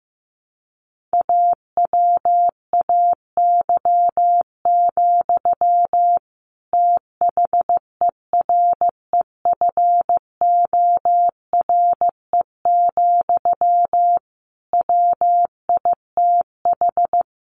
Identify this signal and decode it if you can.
Morse code, heard as a keyed tone, about 15 wpm, 700 Hz. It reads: AWAY, THEREFORE, WITH
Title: Sentences from Top 1500 Words 15wpm